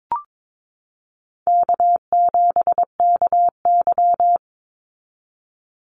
Morse code, heard as a keyed tone, about 22 wpm, 700 Hz. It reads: K7KY
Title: Call Signs Repeated 3x - 22wpm